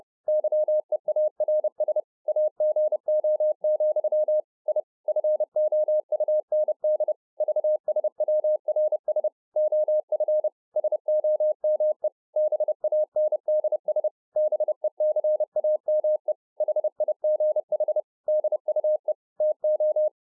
Morse code, heard as a keyed tone, about 30 wpm, 610 Hz. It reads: YEARS AGO, I FOUND VSWRS OF SOME BANDS BECAME HIGH DUE TO